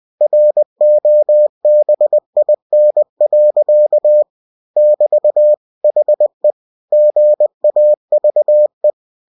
Morse code, heard as a keyed tone, about 20 wpm, 600 Hz. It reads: ROBIN. <BT> HE GAVE
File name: cw-books / winnie-the-pooh / winnie-20wpm-page-0110